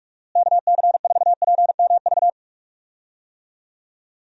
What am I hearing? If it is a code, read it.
Morse code, heard as a keyed tone, about 46 wpm, 700 Hz. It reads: KC4PMV
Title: USA Call Signs - 6 Characters - Repeated 3x - 45wpm